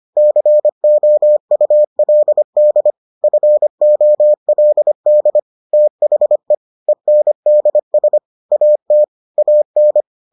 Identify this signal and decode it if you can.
Morse code, heard as a keyed tone, about 25 wpm, 600 Hz. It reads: COULD FOLD THE ENDS AT AN